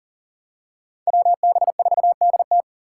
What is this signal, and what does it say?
Morse code, heard as a keyed tone, about 40 wpm, 700 Hz. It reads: WB4DT
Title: USA Call Signs - 5 Characters - 6x Character Spacing - 40wpm